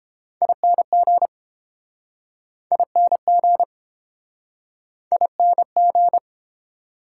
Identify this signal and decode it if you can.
Morse code, heard as a keyed tone, about 29 wpm, 700 Hz. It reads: ING ING ING